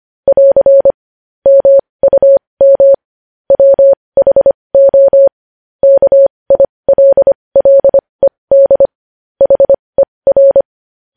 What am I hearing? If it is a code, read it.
Morse code, heard as a keyed tone, about 25 wpm, spaced as for 20 wpm, 550 Hz. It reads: <AR> MUM WHO KILLED HER
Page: News headlines in morse code at 20 wpm